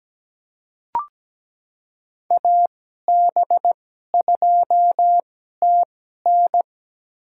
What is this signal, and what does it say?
Morse code, heard as a keyed tone, about 17 wpm, spaced as for 12 wpm, 700 Hz. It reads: AB2TN
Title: USA Call Signs - 5 Characters - 2x Character Spacing - 17wpm